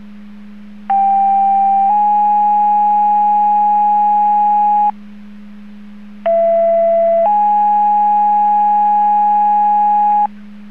Two-tone page, heard as a one-second tone, then a three-second tone, about 790 then 830 Hz, about 670 then 830 Hz. It is heard from 0.9 to 10.3 s.